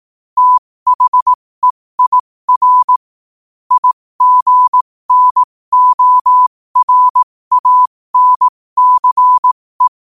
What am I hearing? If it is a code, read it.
Morse code, heard as a keyed tone, about 18 wpm, spaced as for 15 wpm, 1000 Hz. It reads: THEIR IGNORANCE
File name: qotdfiles / 15wpm